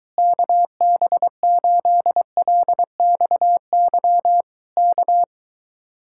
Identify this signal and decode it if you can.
Morse code, heard as a keyed tone, about 23 wpm, 700 Hz. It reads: KB8LXY K